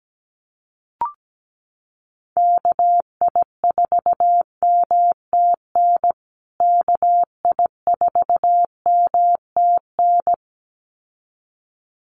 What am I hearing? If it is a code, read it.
Morse code, heard as a keyed tone, about 17 wpm, 700 Hz. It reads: KI4MTN KI4MTN
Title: USA Call Signs Repeated 2x 17wpm